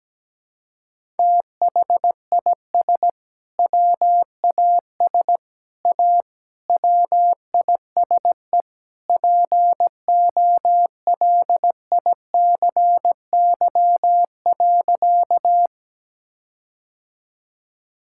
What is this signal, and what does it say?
Morse code, heard as a keyed tone, about 17 wpm, 700 Hz. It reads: THIS WAS A WISE POLICY.
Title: Sentences from Top 1600 Words 17wpm